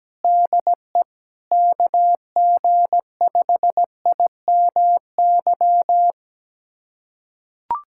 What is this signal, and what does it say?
Morse code, heard as a keyed tone, about 17 wpm, 700 Hz. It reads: DE KG5IMY
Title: USA Callsign DE Callsign - 6 Characters - 17wpm